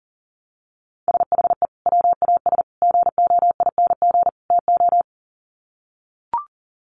Morse code, heard as a keyed tone, about 40 wpm, 700 Hz. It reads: SHE WAS GOING TO
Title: Sets of 4 Words 40wpm